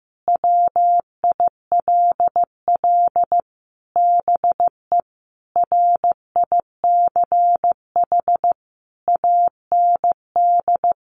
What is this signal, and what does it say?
Morse code, heard as a keyed tone, about 15 wpm, 700 Hz. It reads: WILL BE RICH AND
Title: Sentences from Top 1100 Words 15wpm